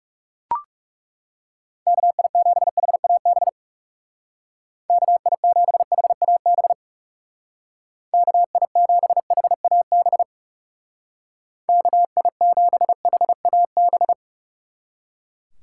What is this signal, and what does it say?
Morse code, heard as a keyed tone, about 36 wpm, 700 Hz. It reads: KI7HAB KI7HAB KI7HAB KI7HAB
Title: Speed Racing - USA 2x3 Call Signs 30wpm